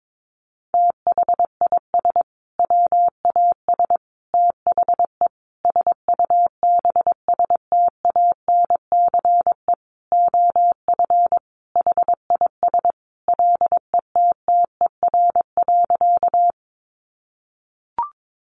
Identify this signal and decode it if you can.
Morse code, heard as a keyed tone, about 22 wpm, 700 Hz. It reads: THIS WAS THE SUBSTANCE OF HIS LETTER.
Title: Sentences from Top 1500 Words 22wpm